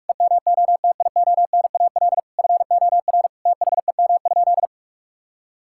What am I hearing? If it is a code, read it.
Morse code, heard as a keyed tone, about 46 wpm, 700 Hz. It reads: EMOTIONAL FOR THEM?
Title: Sentences from Top 1000 Words - Review Part 5 - 45wpm